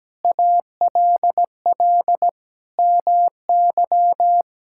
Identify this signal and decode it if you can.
Morse code, heard as a keyed tone, about 17 wpm, 700 Hz. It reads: ALL MY
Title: Sentences from Top 1000 Words - Review Part 6 - 17wpm